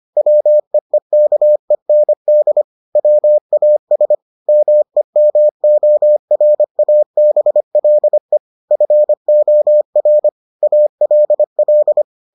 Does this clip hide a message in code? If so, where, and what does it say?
Yes: Morse code from 0.2 to 12.0 s, heard as a keyed tone, about 25 wpm, 600 Hz. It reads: WEEKEND WAS MEMORABLE FOR ALL